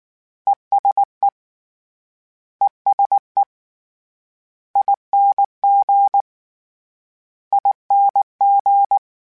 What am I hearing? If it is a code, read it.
Morse code, heard as a keyed tone, about 19 wpm, 800 Hz. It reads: ESE ESE ING ING
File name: Short Runs / Short Runs-mp3s / ESE ING @18